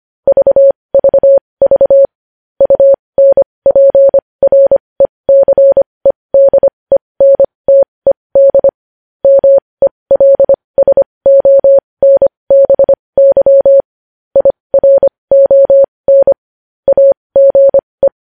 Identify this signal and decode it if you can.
Morse code, heard as a keyed tone, about 25 wpm, spaced as for 20 wpm, 550 Hz. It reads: VVV UNPRECEDENTED MELSONBY IRON AGE
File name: podcast / News headlines-1x-25x20